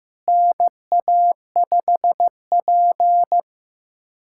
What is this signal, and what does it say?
Morse code, heard as a keyed tone, about 15 wpm, 700 Hz. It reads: NA5P
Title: USA Callsign DE Callsign - 4 Characters - 15wpm